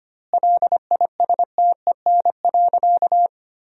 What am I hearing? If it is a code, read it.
Morse code, heard as a keyed tone, about 25 wpm, 700 Hz. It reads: LISTEN.
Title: 2x Word-Spacing - Sentences from Top 500 Words - 25wpm